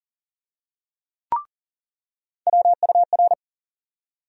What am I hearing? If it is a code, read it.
Morse code, heard as a keyed tone, about 37 wpm, 700 Hz. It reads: WUR
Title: USA Call Sign Postfixes - 3 Letters 40wpm